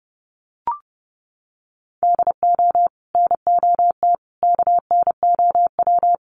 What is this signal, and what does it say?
Morse code, heard as a keyed tone, about 30 wpm, 700 Hz. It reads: DO NOT KNOW
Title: Sets of 4 Words 30wpm